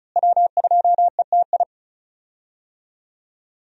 Morse code, heard as a keyed tone, about 34 wpm, 700 Hz. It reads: W2ETI
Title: USA Call Signs - 5 Characters - Repeated 2x - 35wpm